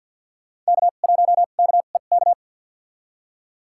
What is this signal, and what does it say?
Morse code, heard as a keyed tone, about 53 wpm, spaced as for 36 wpm, 700 Hz. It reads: K1KEK
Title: USA Call Signs - 5 Characters - 2x Character Spacing - 50wpm